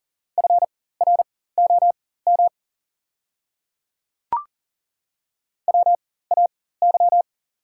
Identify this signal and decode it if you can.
Morse code, heard as a keyed tone, about 37 wpm, spaced as for 18 wpm, 700 Hz. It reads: FROM WAY
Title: 4x Character Spacing - Top 100 Words - 40wpm